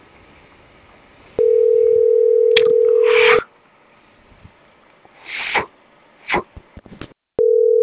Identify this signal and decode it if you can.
Ringback tone (North America), 440 + 480 Hz.